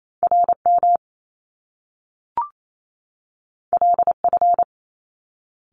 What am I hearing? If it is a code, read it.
Morse code, heard as a keyed tone, about 28 wpm, 700 Hz. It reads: RM LF
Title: Top Two Letter Combinations in Words 28wpm